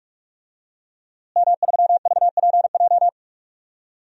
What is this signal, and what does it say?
Morse code, heard as a keyed tone, about 46 wpm, 700 Hz. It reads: M3VPJ
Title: DXCC-100 Call Signs - 5 Characters - Repeated 2x - 45wpm